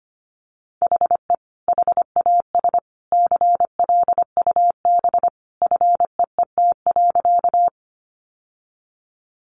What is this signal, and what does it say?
Morse code, heard as a keyed tone, about 25 wpm, 700 Hz. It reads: HE HAS CLUB FEET.
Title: Sentences from Top 1200 Words 25wpm